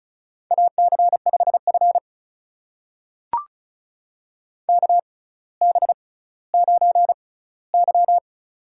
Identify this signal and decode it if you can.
Morse code, heard as a keyed tone, about 36 wpm, spaced as for 18 wpm, 700 Hz. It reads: AC5F KB9Y
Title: USA Call Signs - 4 Characters - 6x Character Spacing - 35wpm